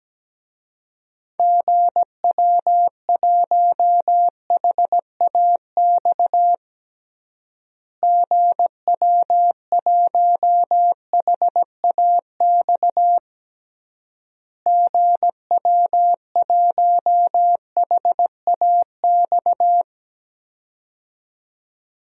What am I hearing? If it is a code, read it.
Morse code, heard as a keyed tone, about 17 wpm, 700 Hz. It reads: GW1HAX GW1HAX GW1HAX
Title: DXCC-100 Call Signs - 6 Characters - Repeated 3x - 17wpm